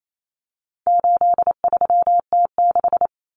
Morse code, heard as a keyed tone, about 28 wpm, 700 Hz. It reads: 83T6
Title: Common Cut Numbers - 4-Digit 28wpm